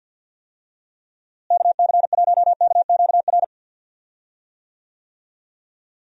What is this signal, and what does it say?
Morse code, heard as a keyed tone, about 51 wpm, 700 Hz. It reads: KC1KQR
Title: USA Call Signs - 6 Characters - Repeated 2x - 50wpm